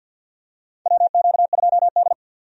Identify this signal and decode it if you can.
Morse code, heard as a keyed tone, about 52 wpm, 700 Hz. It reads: WQ2D
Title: USA Call Signs - 4 Characters - 50wpm